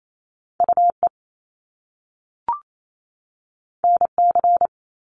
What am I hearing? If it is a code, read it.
Morse code, heard as a keyed tone, about 28 wpm, 700 Hz. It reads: UE NC
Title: Top Two Letter Combinations in Words 28wpm